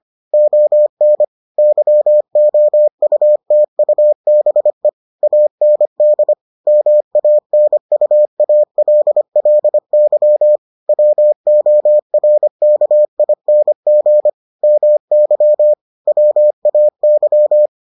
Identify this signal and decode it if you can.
Morse code, heard as a keyed tone, about 25 wpm, 600 Hz. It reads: ON YOUTUBE AND MANUALLY WORKING MY WAY